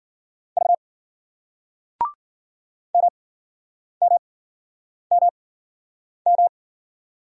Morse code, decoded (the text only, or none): V MMMM